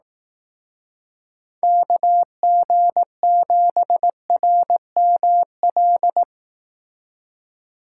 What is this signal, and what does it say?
Morse code, heard as a keyed tone, about 18 wpm, 700 Hz. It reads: KG7RML
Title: Speed Racing - USA 2x3 Call Signs 15wpm